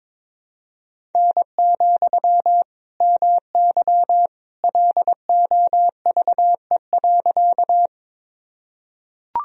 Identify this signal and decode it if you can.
Morse code, heard as a keyed tone, about 22 wpm, 700 Hz. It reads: N, MY LOVE.
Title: Sentences from Top 400 Words and Abbreviations 22wpm